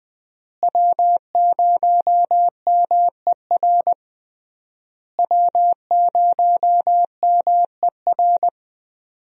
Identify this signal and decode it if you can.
Morse code, heard as a keyed tone, about 20 wpm, 700 Hz. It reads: W0MER W0MER
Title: USA Call Signs - 5 Characters - Repeated 3x - 20wpm